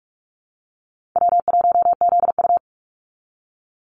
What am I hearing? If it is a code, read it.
Morse code, heard as a keyed tone, about 44 wpm, 700 Hz. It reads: W1ZU
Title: NA SOTA Activator Callsigns 45wpm